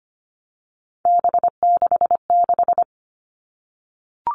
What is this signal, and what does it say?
Morse code, heard as a keyed tone, about 25 wpm, 700 Hz. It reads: B66
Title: All Cut Numbers - 3-Digit 25wpm